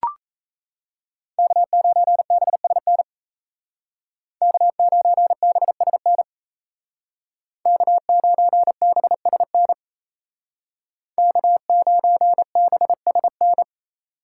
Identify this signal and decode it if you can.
Morse code, heard as a keyed tone, about 36 wpm, 700 Hz. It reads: K9BSN K9BSN K9BSN K9BSN